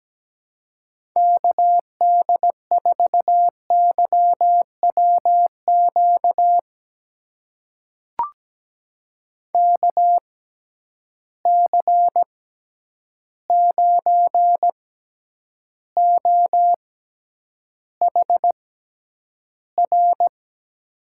Morse code, heard as a keyed tone, about 17 wpm, spaced as for 9 wpm, 700 Hz. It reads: KD4YWQ KC9OHR
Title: USA Call Signs - 6 Characters - 6x Character Spacing - 17wpm